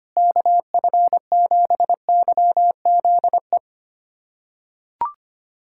Morse code, heard as a keyed tone, about 25 wpm, 700 Hz. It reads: KF7YZE
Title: USA Callsign DE Callsign 25wpm